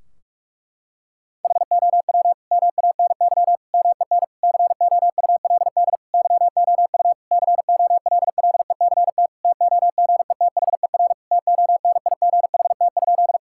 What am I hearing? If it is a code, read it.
Morse code, heard as a keyed tone, about 46 wpm, 700 Hz. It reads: HOW MANY MEN COULD YOU COLLECT TOGETHER TONIGHT?
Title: Sentences from Top 1600 Words 45wpm